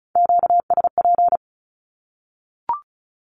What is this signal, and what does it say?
Morse code, heard as a keyed tone, about 34 wpm, 700 Hz. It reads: QSP
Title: Q-Codes Repeated 3x 35wpm